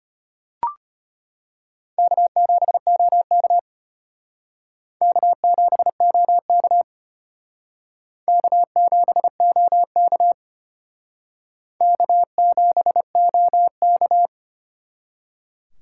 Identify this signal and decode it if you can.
Morse code, heard as a keyed tone, about 32 wpm, 700 Hz. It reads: K7OK K7OK K7OK K7OK